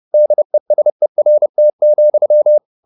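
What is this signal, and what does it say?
Morse code, heard as a keyed tone, about 30 wpm, 600 Hz. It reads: DESERT,
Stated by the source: Morse code at 30 wpm, the novel Never by Ken Follett in English, from CW med Gnister.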